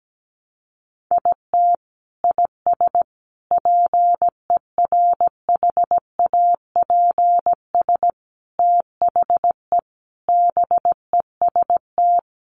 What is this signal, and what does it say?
Morse code, heard as a keyed tone, about 17 wpm, 700 Hz. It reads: IT IS PERHAPS THE BEST
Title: Sentences from Top 1300 Words 17wpm